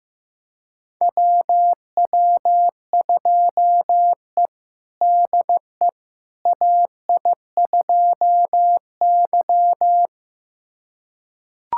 Morse code, heard as a keyed tone, about 16 wpm, 700 Hz. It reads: WW2E DE AI2Y
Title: USA Callsign DE Callsign - 4 Characters - 15wpm